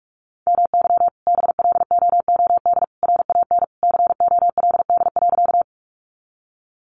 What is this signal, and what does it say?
Morse code, heard as a keyed tone, about 44 wpm, 700 Hz. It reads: MY BLOOD RAN COLD.
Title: Sentences from Top 1100 Words 45wpm